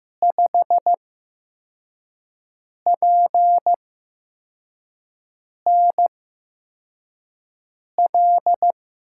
Morse code, heard as a keyed tone, about 15 wpm, spaced as for 4 wpm, 700 Hz. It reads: 5PNL
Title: USA Call Signs - 5 Characters - 8x Character Spacing - 15wpm